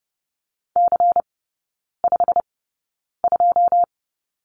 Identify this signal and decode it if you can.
Morse code, heard as a keyed tone, about 30 wpm, spaced as for 9 wpm, 700 Hz. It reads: C52
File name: Lesson 18e - Letter m - Three Characters 30wpm